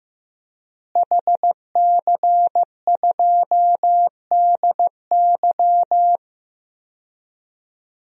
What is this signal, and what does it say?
Morse code, heard as a keyed tone, about 16 wpm, 700 Hz. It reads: HC2DY
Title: DXCC-100 Call Signs - 5 Characters - Repeated 3x - 15wpm